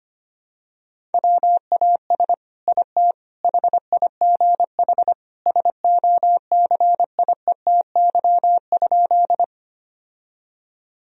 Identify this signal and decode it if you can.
Morse code, heard as a keyed tone, about 25 wpm, 700 Hz. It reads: WAS IT HIGH SOCIETY?
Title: Sentences from Top 1000 Words - Review Part 4 - 25wpm